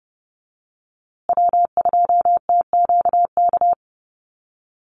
Morse code, heard as a keyed tone, about 30 wpm, 700 Hz. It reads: W2TQK
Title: USA Call Signs Repeated 3x 30wpm